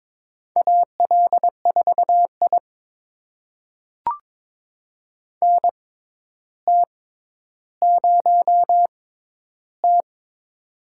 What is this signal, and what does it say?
Morse code, heard as a keyed tone, about 22 wpm, spaced as for 11 wpm, 700 Hz. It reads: AL4I NT0T
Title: USA Call Signs - 6x Character Spacing - 22wpm